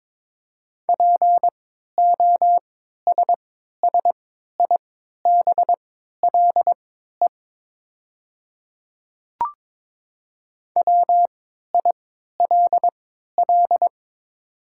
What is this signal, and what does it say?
Morse code, heard as a keyed tone, about 23 wpm, spaced as for 13 wpm, 700 Hz. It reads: POSSIBLE WILL
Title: 3x Character Spacing - Top 500 Words - 22wpm